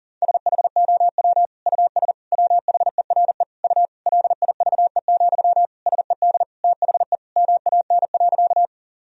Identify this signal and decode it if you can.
Morse code, heard as a keyed tone, about 37 wpm, 700 Hz. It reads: SHOW US WHERE U LIVE, SED THE MAN.
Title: Sentences from Top 200 Words and Abbreviations 40wpm